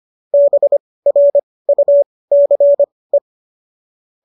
Morse code, heard as a keyed tone, about 25 wpm, spaced as for 18 wpm, 570 Hz. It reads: BRUCE
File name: qso201 18